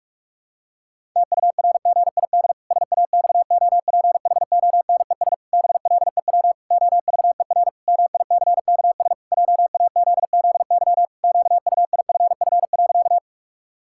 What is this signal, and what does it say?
Morse code, heard as a keyed tone, about 46 wpm, 700 Hz. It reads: TURGID SAXOPHONES BLEW OVER MICKS JAZZY QUIFF.
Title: Pangram Challenge v2 - 45wpm